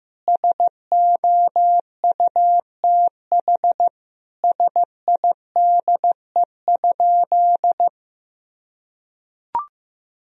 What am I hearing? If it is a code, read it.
Morse code, heard as a keyed tone, about 15 wpm, 700 Hz. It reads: SOUTH SIDE?
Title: Sentences from Top 1000 Words - Review Part 2 - 15wpm